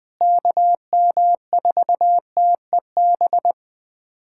Morse code, heard as a keyed tone, about 20 wpm, 700 Hz. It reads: KM4TEB
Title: USA Callsign DE Callsign 20wpm